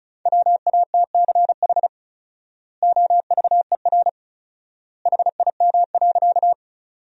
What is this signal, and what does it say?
Morse code, heard as a keyed tone, about 36 wpm, 700 Hz. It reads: WATCH OVER HIM.
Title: 4x Word-Spacing - Sentences from Top 300 Words - 35wpm